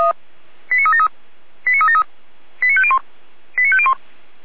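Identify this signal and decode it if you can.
Five-tone selective call (ZVEI), heard as quick runs of five tones, tones of about 70 ms. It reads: 89373 89373 89501 89501
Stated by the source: ZVEI five-tone selective call received on an SDR, from Feuerwehr Orlamünde.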